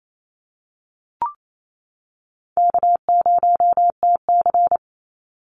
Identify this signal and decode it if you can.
Morse code, heard as a keyed tone, about 28 wpm, 700 Hz. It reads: K0TC
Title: Easy Call Signs - 28wpm